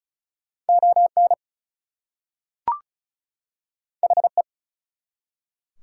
Morse code, heard as a keyed tone, about 33 wpm, 700 Hz. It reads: ON HE
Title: Two Letter Words - 35wpm